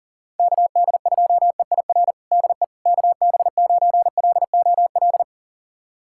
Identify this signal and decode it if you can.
Morse code, heard as a keyed tone, about 40 wpm, 700 Hz. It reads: KD2EIR DE KB9LOL
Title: USA Callsign DE Callsign - 6 Characters - 40wpm